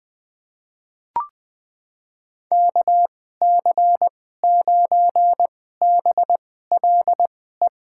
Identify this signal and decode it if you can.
Morse code, heard as a keyed tone, about 20 wpm, spaced as for 14 wpm, 700 Hz. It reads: KC9BLE
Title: USA Call Signs - 6 Characters - 2x Character Spacing - 20wpm